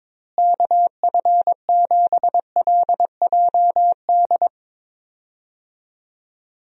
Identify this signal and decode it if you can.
Morse code, heard as a keyed tone, about 22 wpm, 700 Hz. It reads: KF7LJD
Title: USA Call Signs - 6 Characters - Repeated 2x - 22wpm